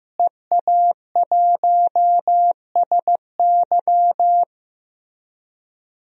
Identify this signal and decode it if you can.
Morse code, heard as a keyed tone, about 16 wpm, 700 Hz. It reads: EA1SY
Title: DXCC-100 Call Signs - 5 Characters - Repeated 3x - 15wpm